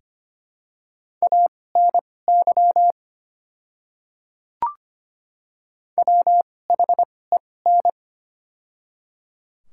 Morse code, heard as a keyed tone, about 25 wpm, spaced as for 18 wpm, 700 Hz. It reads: ANY WHEN
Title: 2x Character Spacing - Top 200 Words - 25wpm